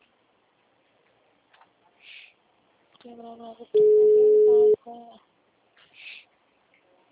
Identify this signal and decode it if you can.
Ringback tone (Europe, 425 Hz), with speech.
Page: Jundiai - SP